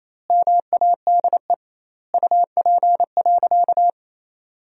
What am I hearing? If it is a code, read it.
Morse code, heard as a keyed tone, about 28 wpm, 700 Hz. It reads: MADE UP.
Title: 2x Word-Spacing - Sentences from Top 500 Words - 28wpm